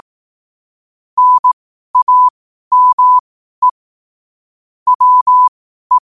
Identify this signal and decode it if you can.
Morse code, heard as a keyed tone, about 18 wpm, spaced as for 12 wpm, 1000 Hz. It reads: NAME WE
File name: qotdfiles / 12wpm